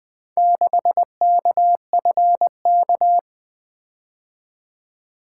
Morse code, heard as a keyed tone, about 20 wpm, 700 Hz. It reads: 6KFK
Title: Four Letter-Number Combinations - 20wpm